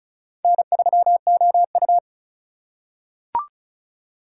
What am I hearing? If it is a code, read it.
Morse code, heard as a keyed tone, about 36 wpm, 700 Hz. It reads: N3OU
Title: USA Call Signs - 4 Characters - 3x Character Spacing - 35wpm